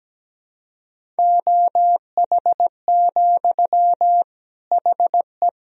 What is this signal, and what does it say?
Morse code, heard as a keyed tone, about 17 wpm, 700 Hz. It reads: OH, HE
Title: Sentences from Top 500 Words and Abbreviations 17wpm